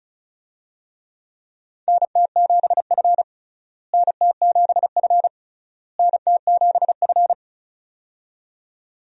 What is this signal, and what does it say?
Morse code, heard as a keyed tone, about 36 wpm, 700 Hz. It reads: NT7F NT7F NT7F
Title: USA Call Signs - 4 Characters - Repeated 3x - 35wpm